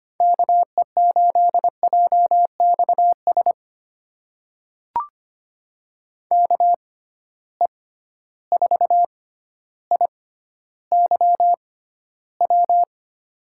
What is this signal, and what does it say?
Morse code, heard as a keyed tone, about 25 wpm, spaced as for 13 wpm, 700 Hz. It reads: KE8JXS KE4IYW
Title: USA Call Signs - 6 Characters - 6x Character Spacing - 25wpm